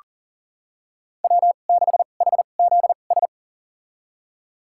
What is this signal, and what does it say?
Morse code, heard as a keyed tone, about 37 wpm, spaced as for 28 wpm, 700 Hz. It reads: W6HZS
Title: USA Call Signs - 2x Character Spacing - 40wpm